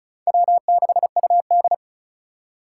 Morse code, heard as a keyed tone, about 45 wpm, 700 Hz. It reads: W6UD